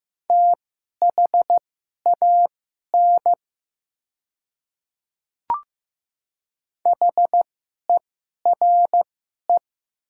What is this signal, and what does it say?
Morse code, heard as a keyed tone, about 15 wpm, spaced as for 11 wpm, 700 Hz. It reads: THAN HERE
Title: 2x Character Spacing - Top 300 Words - 15wpm